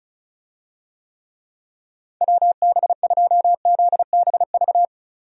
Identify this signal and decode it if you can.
Morse code, heard as a keyed tone, about 34 wpm, 700 Hz. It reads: WB2ZBV
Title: USA Call Signs - 6 Characters - 35wpm